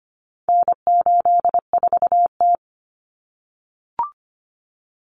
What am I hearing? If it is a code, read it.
Morse code, heard as a keyed tone, about 25 wpm, 700 Hz. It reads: N84T
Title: Common Cut Numbers - 4-Digit 25wpm